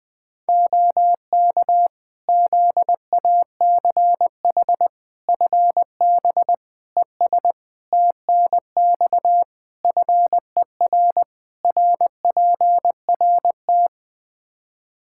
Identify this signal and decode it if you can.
Morse code, heard as a keyed tone, about 20 wpm, 700 Hz. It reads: OK ZACH FB ES TNX FER RPRT
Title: Practice QSO - Exchange 2 - Info - 20wpm